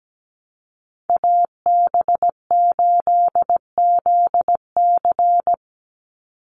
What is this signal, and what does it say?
Morse code, heard as a keyed tone, about 17 wpm, 700 Hz. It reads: AB8ZC
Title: USA Call Signs 17wpm